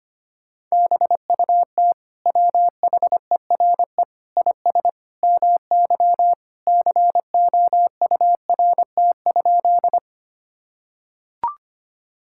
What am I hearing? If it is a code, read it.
Morse code, heard as a keyed tone, about 25 wpm, 700 Hz. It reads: BUT WHERE IS MY COURT?
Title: Sentences from Top 1600 Words 25wpm